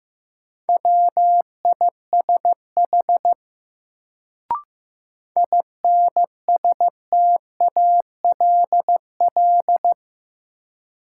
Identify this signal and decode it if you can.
Morse code, heard as a keyed tone, about 15 wpm, 700 Hz. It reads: WISH INSTALL
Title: Sentences from Top 1600 Words - New Word Drill Alt - 15wpm